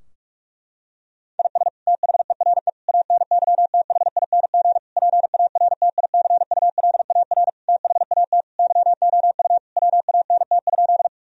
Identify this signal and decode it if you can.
Morse code, heard as a keyed tone, about 46 wpm, 700 Hz. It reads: IS THERE ANYTHING PARTICULAR THAT YOU WANT?
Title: Sentences from Top 1000 Words - Review Part 4 - 45wpm